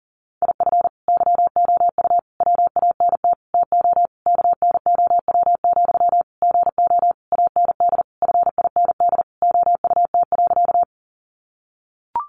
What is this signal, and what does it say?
Morse code, heard as a keyed tone, about 40 wpm, 700 Hz. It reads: IF YOU WANT TO KNOW, GO AND FIND OUT.
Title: Sentences from Top 200 Words 40wpm